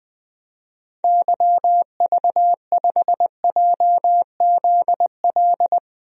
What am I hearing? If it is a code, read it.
Morse code, heard as a keyed tone, about 20 wpm, 700 Hz. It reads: YV5JZL
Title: DXCC-100 Call Signs - 6 Characters - Repeated 2x - 20wpm